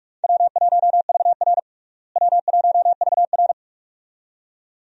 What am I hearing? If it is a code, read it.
Morse code, heard as a keyed tone, about 46 wpm, 700 Hz. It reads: W1VR W1VR
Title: USA Call Signs - 4 Characters - Repeated 2x - 45wpm